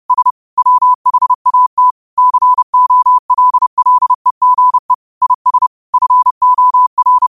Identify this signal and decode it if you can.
Morse code, heard as a keyed tone, about 30 wpm, 1000 Hz. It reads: S WHAT COLLEGE IS FOR
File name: qotdfiles / 30wpm